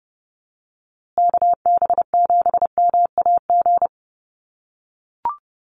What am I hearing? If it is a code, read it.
Morse code, heard as a keyed tone, about 30 wpm, 700 Hz. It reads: KB7MAG
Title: USA Call Signs Repeated 3x 30wpm